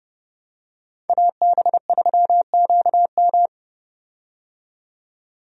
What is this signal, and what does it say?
Morse code, heard as a keyed tone, about 31 wpm, 700 Hz. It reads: AB3QM
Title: USA Call Signs - 5 Characters - 30wpm